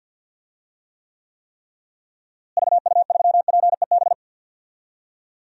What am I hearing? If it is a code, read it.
Morse code, heard as a keyed tone, about 52 wpm, 700 Hz. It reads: VU3PEB
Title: DXCC-100 Call Signs - 6 Characters - Repeated 3x - 50wpm